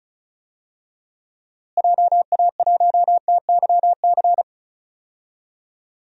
Morse code, heard as a keyed tone, about 36 wpm, 700 Hz. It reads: JA1TYC
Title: DXCC-100 Call Signs - 6 Characters - Repeated 2x - 35wpm